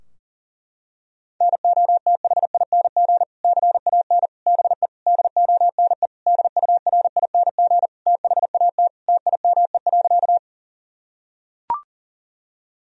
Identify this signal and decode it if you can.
Morse code, heard as a keyed tone, about 38 wpm, 700 Hz. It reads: NOTHING CAN BE DONE DURING THAT TIME.
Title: Sentences from Top 1000 Words - Review Part 2 - 40wpm